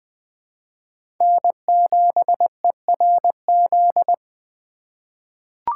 Morse code, heard as a keyed tone, about 20 wpm, 700 Hz. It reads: N7ERZ
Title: USA Call Signs - 5 Characters - 2x Character Spacing - 20wpm